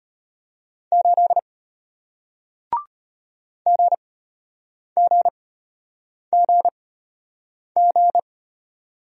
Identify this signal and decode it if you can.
Morse code, heard as a keyed tone, about 34 wpm, spaced as for 8 wpm, 700 Hz. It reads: 8 GGGG